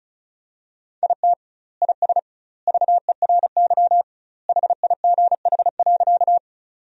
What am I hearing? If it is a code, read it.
Morse code, heard as a keyed tone, about 36 wpm, 700 Hz. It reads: IT IS VERY HIGH.
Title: 2x Word-Spacing - Sentences from Top 200 Words - 35wpm